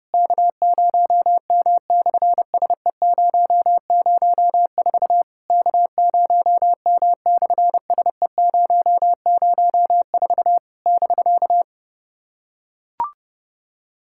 Morse code, heard as a keyed tone, about 30 wpm, 700 Hz. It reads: K0M/SE004 K0M/SE004 <BK>
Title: SOTA QSO Practice 30wpm